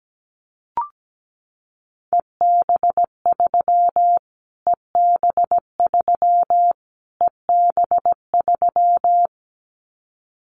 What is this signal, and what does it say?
Morse code, heard as a keyed tone, about 17 wpm, 700 Hz. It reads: EB3 EB3 EB3
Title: Callsign Prefixes for Top 25 DXCC - Repeated 3x - 17wpm